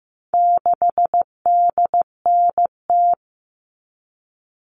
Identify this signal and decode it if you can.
Morse code, heard as a keyed tone, about 15 wpm, 700 Hz. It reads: 6DNT
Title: All Cut Numbers - 4-Digit 15wpm